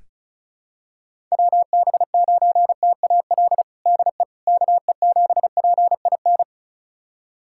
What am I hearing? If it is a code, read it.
Morse code, heard as a keyed tone, about 36 wpm, 700 Hz. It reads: WB9TAL DE KE7PIN
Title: USA Callsign DE Callsign - 6 Characters - 35wpm